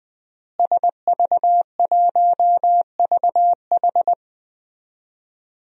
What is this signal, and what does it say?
Morse code, heard as a keyed tone, about 20 wpm, 700 Hz. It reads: SV1VH
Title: DXCC-100 Call Signs - 5 Characters - Repeated 2x - 20wpm